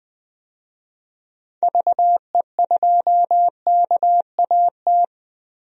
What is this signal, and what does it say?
Morse code, heard as a keyed tone, about 20 wpm, 700 Hz. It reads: VE2KAT
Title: DXCC-100 Call Signs - 6 Characters - Repeated 2x - 20wpm